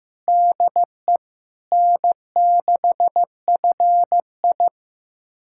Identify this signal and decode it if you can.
Morse code, heard as a keyed tone, about 15 wpm, 700 Hz. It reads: DE N6FI
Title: USA Callsign DE Callsign - 4 Characters - 15wpm